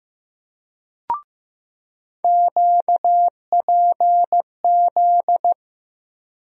Morse code, heard as a keyed tone, about 15 wpm, 700 Hz. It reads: QPZ
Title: USA Call Sign Postfixes - 3 Letters 15wpm